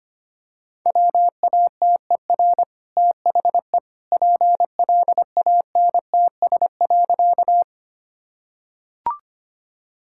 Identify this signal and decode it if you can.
Morse code, heard as a keyed tone, about 25 wpm, 690 Hz. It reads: WATER THE PLANTS.
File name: Easy Sentences with Hints - 25wpm